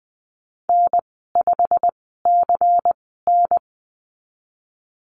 Morse code, heard as a keyed tone, about 20 wpm, spaced as for 15 wpm, 700 Hz. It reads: N5CN
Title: 173 Lesson 28g - Number 9 - Call Signs 20wpm